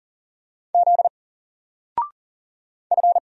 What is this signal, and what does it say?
Morse code, heard as a keyed tone, about 39 wpm, 700 Hz. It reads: Z F
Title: Single Character (Letter:Number:Punctuation:Prosign) 40wpm